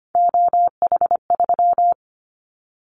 Morse code, heard as a keyed tone, about 25 wpm, 700 Hz. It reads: OH3
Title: Callsign Prefixes for Top 50 DXCC - 25wpm